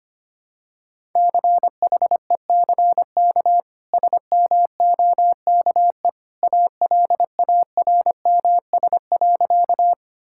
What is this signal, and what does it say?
Morse code, heard as a keyed tone, about 25 wpm, 700 Hz. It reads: CHECK SMOKE ALARMS.
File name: Easy Sentences with Hints - 25wpm